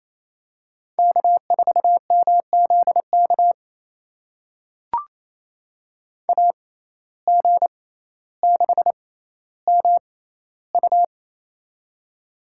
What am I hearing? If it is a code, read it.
Morse code, heard as a keyed tone, about 29 wpm, spaced as for 15 wpm, 700 Hz. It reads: K4MZK AG6MU
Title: USA Call Signs - 5 Characters - 6x Character Spacing - 28wpm